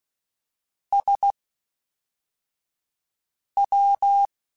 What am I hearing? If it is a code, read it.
Morse code, heard as a keyed tone, about 16 wpm, 800 Hz. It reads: S W
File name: Misc / Misc-mp3s / Jumps @10